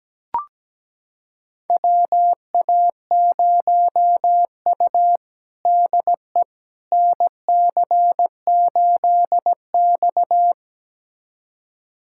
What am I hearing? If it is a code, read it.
Morse code, heard as a keyed tone, about 17 wpm, 700 Hz. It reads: WA0U DE NC8X
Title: USA Callsign DE Callsign - 4 Characters - 17wpm